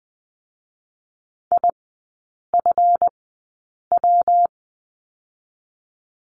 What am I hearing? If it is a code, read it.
Morse code, heard as a keyed tone, about 20 wpm, spaced as for 8 wpm, 700 Hz. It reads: IFW
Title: 138 Lesson 23e - Letter f - Three Characters 20wpm